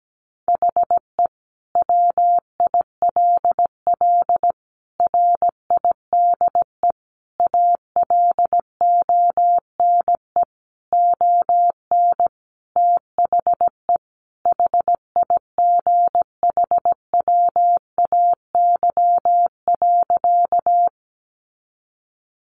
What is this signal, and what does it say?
Morse code, heard as a keyed tone, about 17 wpm, 700 Hz. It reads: HE WILL RIDE ALONE ON THE HIGHWAY.
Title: Sentences from Top 1400 Words 17wpm